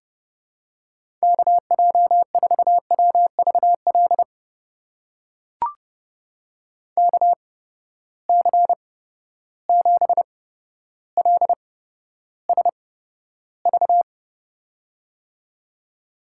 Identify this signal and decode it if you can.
Morse code, heard as a keyed tone, about 30 wpm, spaced as for 13 wpm, 700 Hz. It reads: KJ4WVL KC7LSV
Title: USA Call Signs - 6 Characters - 8x Character Spacing - 30wpm